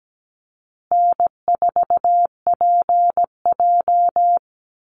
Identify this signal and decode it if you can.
Morse code, heard as a keyed tone, about 17 wpm, 700 Hz. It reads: N4PJ
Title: Call Signs Repeated 3x - 17wpm